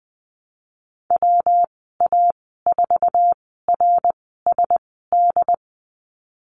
Morse code, heard as a keyed tone, about 20 wpm, spaced as for 15 wpm, 700 Hz. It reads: WA4RSD
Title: 186 Lesson 30g - Letter b - Call Signs 20wpm